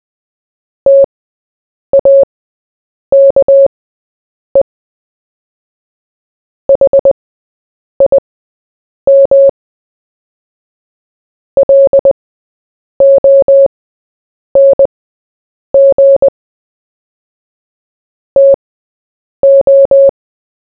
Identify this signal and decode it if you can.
Morse code, heard as a keyed tone, about 20 wpm, spaced as for 8 wpm, 550 Hz. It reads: TAKE HIM LONG TO
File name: Short-Story-4-8wpm